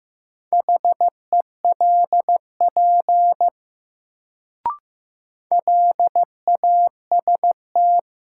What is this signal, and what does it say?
Morse code, heard as a keyed tone, about 15 wpm, 700 Hz. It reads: HELP LAST
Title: Sets of 2 Words - New Word Drill - 15wpm